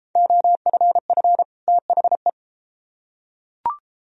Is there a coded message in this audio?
Yes: Morse code from 0.2 to 2.3 s, heard as a keyed tone, about 30 wpm, 700 Hz. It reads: OFF THE